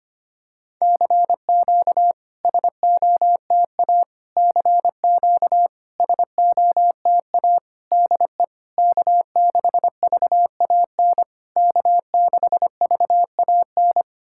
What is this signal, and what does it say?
Morse code, heard as a keyed tone, about 25 wpm, 660 Hz. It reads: CQ SOTA CQ SOTA DE K6VAN K6VAN
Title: SOTA QSO Practice 25wpm